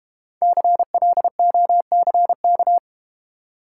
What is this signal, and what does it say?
Morse code, heard as a keyed tone, about 31 wpm, 690 Hz. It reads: CLOCK
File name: Easy Words - Speed-Racing with Hints - 25wpm